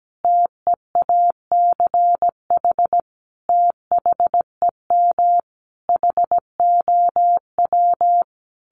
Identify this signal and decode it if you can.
Morse code, heard as a keyed tone, about 17 wpm, 700 Hz. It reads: TEACH THEM HOW
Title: Sentences from Top 1300 Words 17wpm